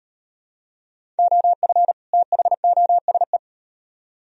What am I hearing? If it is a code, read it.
Morse code, heard as a keyed tone, about 38 wpm, 700 Hz. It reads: OF THOSE